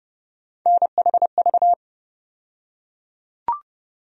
Morse code, heard as a keyed tone, about 31 wpm, 700 Hz. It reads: NHV
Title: USA Call Sign Postfixes - 3 Letters 30wpm